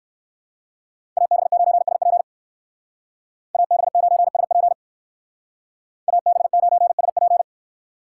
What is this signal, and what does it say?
Morse code, heard as a keyed tone, about 58 wpm, 700 Hz. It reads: AB9SP AB9SP AB9SP